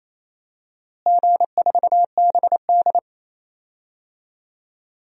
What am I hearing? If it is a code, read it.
Morse code, heard as a keyed tone, about 27 wpm, 700 Hz. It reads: G4BD
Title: DXCC-100 Call Signs - 4 Characters - Repeated 3x - 28wpm